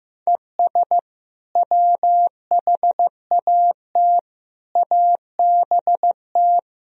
Morse code, heard as a keyed tone, about 15 wpm, 700 Hz. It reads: ES WHAT ABT
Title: Sentences from Top 200 Words and Abbreviations 15wpm